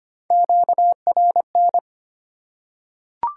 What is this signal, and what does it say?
Morse code, heard as a keyed tone, about 25 wpm, 700 Hz. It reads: QRN
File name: Top 90 QSO Elements - 6x Character Spacing - 25wpm